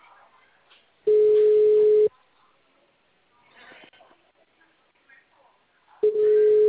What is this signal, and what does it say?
Ringback tone (Europe, 425 Hz), with speech.
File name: out- -2015- - - .10